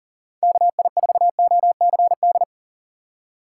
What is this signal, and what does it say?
Morse code, heard as a keyed tone, about 37 wpm, 700 Hz. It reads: KI4OCD
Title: USA Call Signs - 3x Character Spacing - 40wpm